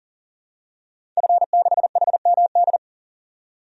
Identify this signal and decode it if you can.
Morse code, heard as a keyed tone, about 37 wpm, 700 Hz. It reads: F6HMD
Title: DXCC-100 Call Signs - 5 Characters - 40wpm